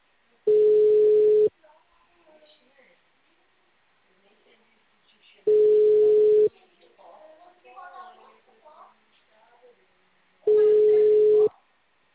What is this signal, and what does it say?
Ringback tone (Europe, 425 Hz), with speech.
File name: out- -2004- - - .185